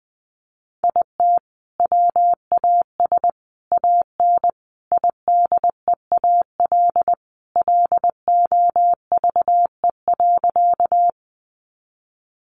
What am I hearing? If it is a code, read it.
Morse code, heard as a keyed tone, about 20 wpm, 700 Hz. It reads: IT WAS AN IDEAL LOVE.
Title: Sentences from Top 1400 Words 20wpm